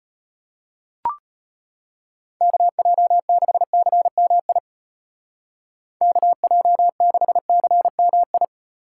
Morse code, heard as a keyed tone, about 30 wpm, 700 Hz. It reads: KJ6CMI KJ6CMI
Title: Speed Racing - USA Call Signs 25wpm